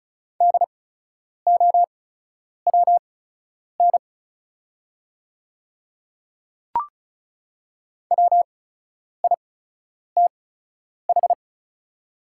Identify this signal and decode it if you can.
Morse code, heard as a keyed tone, about 36 wpm, spaced as for 10 wpm, 700 Hz. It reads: DOWN WITH
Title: 8x Character Spacing - Top 100 Words - 35wpm